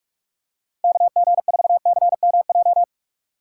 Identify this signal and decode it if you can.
Morse code, heard as a keyed tone, about 47 wpm, 700 Hz. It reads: KG4CMJ
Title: USA Call Signs - 6 Characters - 6x Character Spacing - 45wpm